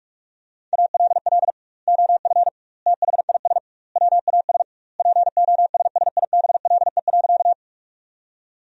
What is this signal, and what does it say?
Morse code, heard as a keyed tone, about 46 wpm, 700 Hz. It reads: ALL OF THIS WAS POSSIBLE.
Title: 2x Word-Spacing - Sentences from Top 500 Words - 45wpm